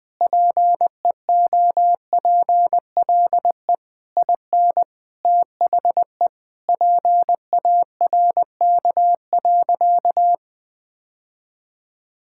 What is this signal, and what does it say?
Morse code, heard as a keyed tone, about 21 wpm, 700 Hz. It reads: PEOPLE IN THE PARK.
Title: Sentences with 3-Word Phrases - 20wpm